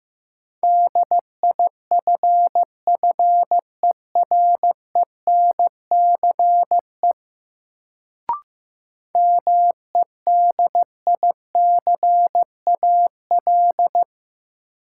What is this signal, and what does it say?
Morse code, heard as a keyed tone, about 15 wpm, 700 Hz. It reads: DIFFERENCE MEDICAL
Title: Sentences from Top 800 Words - New Word Drill - 15wpm